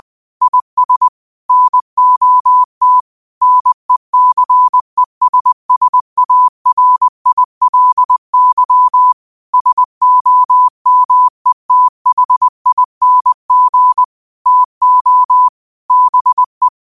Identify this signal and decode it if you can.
Morse code, heard as a keyed tone, about 20 wpm, 1000 Hz. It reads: IS NOT NECESSARILY SOMETHING TO BE
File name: 2025-06-25-20wpm